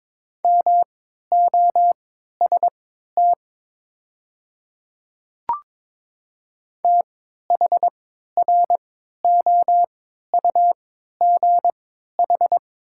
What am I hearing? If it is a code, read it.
Morse code, heard as a keyed tone, about 22 wpm, spaced as for 13 wpm, 700 Hz. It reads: MOST THROUGH
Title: 3x Character Spacing - Top 200 Words - 22wpm